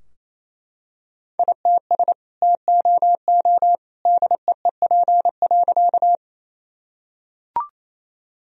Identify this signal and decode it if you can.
Morse code, heard as a keyed tone, about 28 wpm, 700 Hz. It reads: ITS TOO DEEP.
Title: Sentences from Top 1000 Words - Review Part 3 - 28wpm